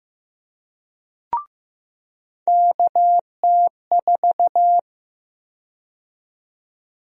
Morse code, heard as a keyed tone, about 15 wpm, 700 Hz. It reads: KT4
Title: USA Call Sign Prefixes - 15wpm